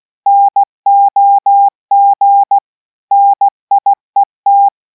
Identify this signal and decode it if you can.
Morse code, heard as a keyed tone, about 16 wpm, 800 Hz. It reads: NOG NIET